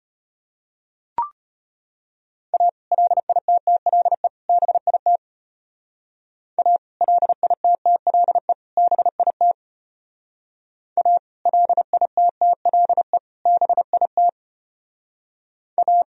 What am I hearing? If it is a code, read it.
Morse code, heard as a keyed tone, about 32 wpm, 700 Hz. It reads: A LITTLE BIT A LITTLE BIT A LITTLE BIT A